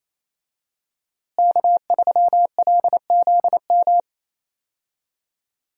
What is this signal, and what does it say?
Morse code, heard as a keyed tone, about 28 wpm, 700 Hz. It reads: K3LZM
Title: USA Call Signs - 5 Characters - Repeated 3x - 28wpm